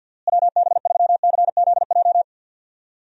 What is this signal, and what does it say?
Morse code, heard as a keyed tone, about 50 wpm, 700 Hz. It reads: WB3CZJ
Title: USA Call Signs - 6 Characters - Repeated 2x - 50wpm